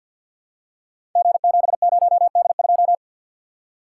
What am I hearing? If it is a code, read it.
Morse code, heard as a keyed tone, about 50 wpm, 700 Hz. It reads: G70D2
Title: Code Groups - 5 Letter:Numbers - 50wpm